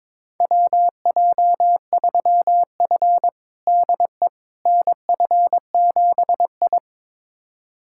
Morse code, heard as a keyed tone, about 22 wpm, 700 Hz. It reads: WJ3F DE NF7I
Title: USA Callsign DE Callsign - 4 Characters - 22wpm